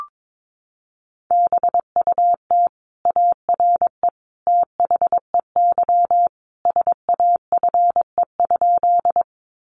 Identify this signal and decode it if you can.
Morse code, heard as a keyed tone, about 22 wpm, 700 Hz. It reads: BUT ARE THEY SAFE?
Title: Sentences from Top 900 Words 22wpm